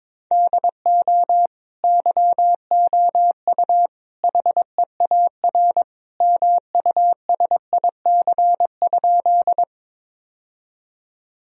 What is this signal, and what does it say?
Morse code, heard as a keyed tone, about 22 wpm, 700 Hz. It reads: DO YOU HEAR MUSIC?
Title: Sentences from Top 1000 Words - Review Part 1 - 22wpm